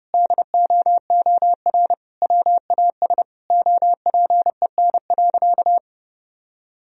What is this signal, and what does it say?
Morse code, heard as a keyed tone, about 30 wpm, 700 Hz. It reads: DOOR WAS OPEN.
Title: Sentences from Top 1000 Words - Review Part 2 - 30wpm